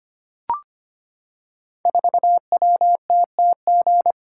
Morse code, heard as a keyed tone, about 25 wpm, 700 Hz. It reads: 4WTTG
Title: Code Groups - 5 Letter:Numbers - 25wpm